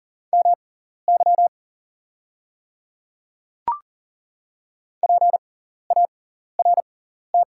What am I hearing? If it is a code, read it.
Morse code, heard as a keyed tone, about 40 wpm, spaced as for 14 wpm, 700 Hz. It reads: MY PART
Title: 6x Character Spacing - Top 300 Words - 40wpm